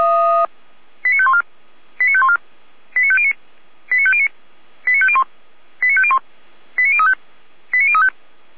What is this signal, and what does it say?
Five-tone selective call (ZVEI), heard as quick runs of five tones, tones of about 70 ms. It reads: 89514 89514 89509 89509 89501 89501 89035 89035
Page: /455/SDR/ZVEI/89001/